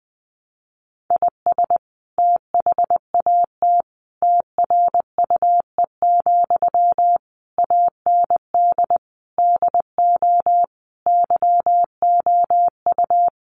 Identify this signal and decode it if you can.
Morse code, heard as a keyed tone, about 20 wpm, 700 Hz. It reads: IS THAT TRUE, AND DO YOU
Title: Sentences from Top 1200 Words 20wpm